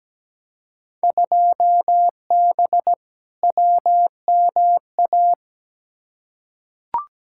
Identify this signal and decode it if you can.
Morse code, heard as a keyed tone, about 17 wpm, 700 Hz. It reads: 2B WMA
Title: Field Day Exchanges Encore - 17wpm